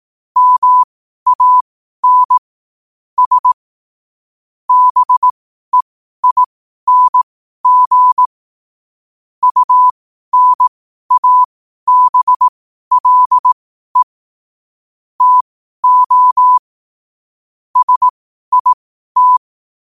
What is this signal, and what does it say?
Morse code, heard as a keyed tone, about 18 wpm, spaced as for 12 wpm, 1000 Hz. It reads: MAN S BEING UNABLE TO SIT
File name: qotdfiles / 12wpm